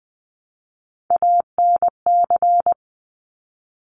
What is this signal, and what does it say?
Morse code, heard as a keyed tone, about 20 wpm, 700 Hz. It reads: ANC
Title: Top Three Letter Combinations in Words 20wpm